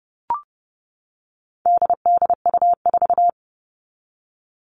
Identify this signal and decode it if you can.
Morse code, heard as a keyed tone, about 30 wpm, 700 Hz. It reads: DDU4
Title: All Cut Numbers - 4-Digit 30wpm